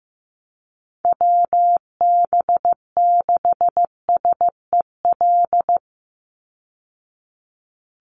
Morse code, heard as a keyed tone, about 15 wpm, 700 Hz. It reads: WB6SEL
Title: Call Signs Repeated 3x - 15wpm